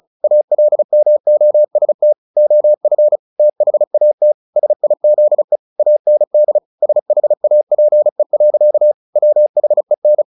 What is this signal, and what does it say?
Morse code, heard as a keyed tone, about 34 wpm, 600 Hz. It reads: ALMOST OF THAT SIZE AND SHAPE. WHEN
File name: the old man and the sea / the old man and the sea 1 35wpm